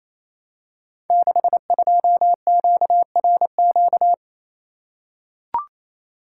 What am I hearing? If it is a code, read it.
Morse code, heard as a keyed tone, about 28 wpm, 700 Hz. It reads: 62QRQ
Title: Code Groups - 5 Letter:Numbers - 28wpm